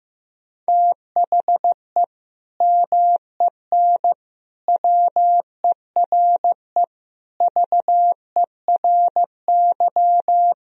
Morse code, heard as a keyed tone, about 16 wpm, 700 Hz. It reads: THE MEN WERE VERY
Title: Sentences from Top 1000 Words - Review Part 3 - 15wpm